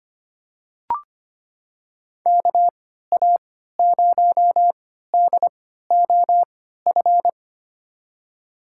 Morse code, heard as a keyed tone, about 25 wpm, spaced as for 14 wpm, 700 Hz. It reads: KA0DOF
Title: USA Call Signs - 6 Characters - 3x Character Spacing - 25wpm